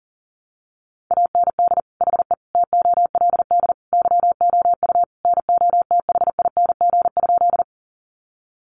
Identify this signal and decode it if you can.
Morse code, heard as a keyed tone, about 40 wpm, 700 Hz. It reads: AND HE TOLD YOU NOTHING?
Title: Sentences from Top 500 Words 40wpm